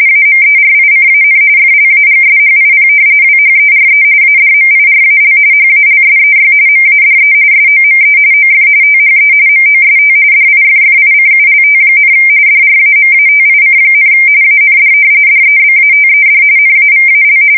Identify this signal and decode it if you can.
RTTY, about 45.45 baud. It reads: AMERICAN FLAGSHIP WHERE ADMIRAL HALSEY MADE SOME OF THE MOST DIFFICULT, DRAMATIC, AND DEBATED NAVAL
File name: fsk170 / battleshipnj